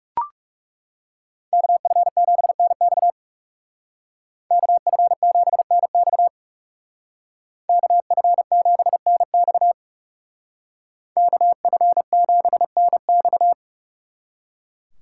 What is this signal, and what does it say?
Morse code, heard as a keyed tone, about 39 wpm, 700 Hz. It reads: KF7NX KF7NX KF7NX KF7NX